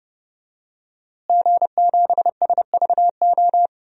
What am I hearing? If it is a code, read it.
Morse code, heard as a keyed tone, about 28 wpm, 700 Hz. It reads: G7SVO
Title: DXCC-100 Call Signs - 5 Characters - Repeated 3x - 30wpm